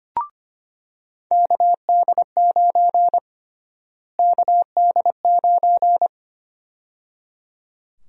Morse code, heard as a keyed tone, about 25 wpm, 700 Hz. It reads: KD9 KD9
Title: USA Call Sign Prefixes - Repeated 2x - 25wpm